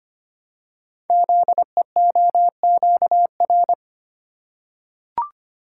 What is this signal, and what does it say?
Morse code, heard as a keyed tone, about 25 wpm, 700 Hz. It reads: ZEOQR
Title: Code Groups - 5 Letters - 25wpm